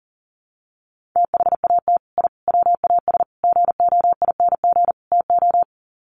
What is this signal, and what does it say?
Morse code, heard as a keyed tone, about 40 wpm, 700 Hz. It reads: THAT I WAS GOING TO
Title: Sets of 5 Words Encore 40wpm